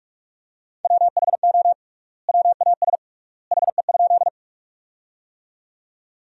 Morse code, heard as a keyed tone, about 46 wpm, 700 Hz. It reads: WHO WAS HE?
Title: 3x Word-Spacing - Sentences from Top 100 Words - 45wpm